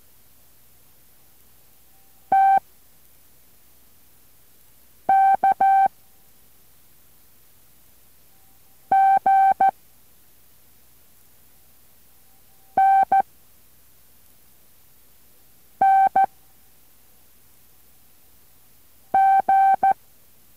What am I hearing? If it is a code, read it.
Morse code, heard as a keyed tone, about 14 wpm, 780 Hz. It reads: T K G N N G